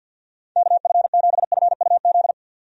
Morse code, heard as a keyed tone, about 49 wpm, 700 Hz. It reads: KF7FUZ
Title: USA Call Signs - 6 Characters - 50wpm